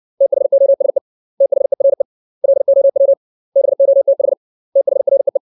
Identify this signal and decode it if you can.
Morse code, heard as a keyed tone, about 60 wpm, 550 Hz. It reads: THOSE THERE LOW BOTH THREE
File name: 632 words / 632 Words 60WPM